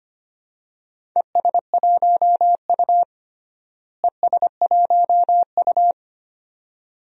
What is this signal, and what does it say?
Morse code, heard as a keyed tone, about 25 wpm, 700 Hz. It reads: ES1U ES1U
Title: DXCC-100 Call Signs - 4 Characters - Repeated 2x - 25wpm